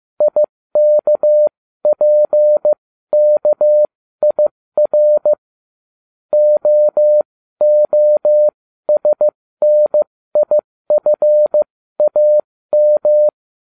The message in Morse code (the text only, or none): IKPKIR OOSNIFAM